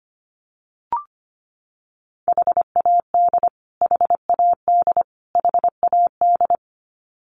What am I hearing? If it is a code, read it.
Morse code, heard as a keyed tone, about 25 wpm, 700 Hz. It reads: HAD HAD HAD
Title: Top 300 Words Repeated 25wpm